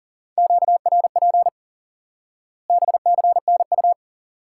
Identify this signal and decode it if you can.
Morse code, heard as a keyed tone, about 40 wpm, 700 Hz. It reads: QRP BCNU
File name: IQE Course 3 - Lesson 89 - YAGI - Two Elements - 40wpm